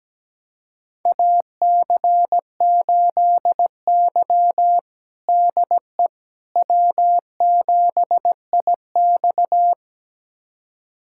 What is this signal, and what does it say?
Morse code, heard as a keyed tone, about 17 wpm, 700 Hz. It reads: AC8Y DE W7IX
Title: USA Callsign DE Callsign - 4 Characters - 17wpm